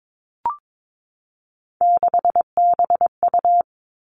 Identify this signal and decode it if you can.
Morse code, heard as a keyed tone, about 22 wpm, 700 Hz. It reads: 6BU
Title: Three Challenging Letter-Number Combinations 22wpm